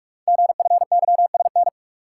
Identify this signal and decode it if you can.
Morse code, heard as a keyed tone, about 44 wpm, 700 Hz. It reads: GFYSN
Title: Code Groups - 5 Letter:Numbers - 45wpm